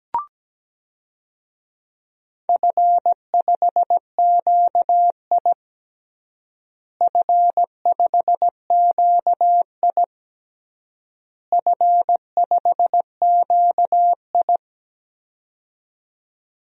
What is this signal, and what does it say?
Morse code, heard as a keyed tone, about 17 wpm, 700 Hz. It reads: F5QI F5QI F5QI
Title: DXCC-100 Call Signs - 4 Characters - Repeated 3x - 17wpm